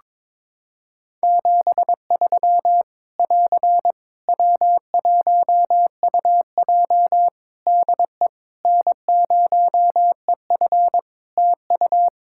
Morse code, heard as a keyed tone, about 22 wpm, 700 Hz. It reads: 73 <AR> W1UJ DE N0EF TU
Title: Practice QSO - Exchange 4 - Ending - 22wpm